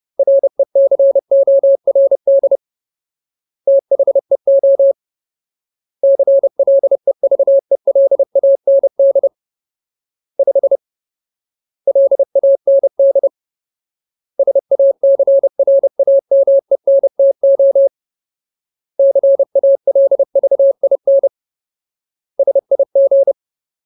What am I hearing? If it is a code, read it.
Morse code, heard as a keyed tone, about 30 wpm, 550 Hz. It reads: RECORD THEO CLEVELAND 5 LAND SACRAMENTO CALVIN SIG
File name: Ultimate Word List / 30WPM Ultimate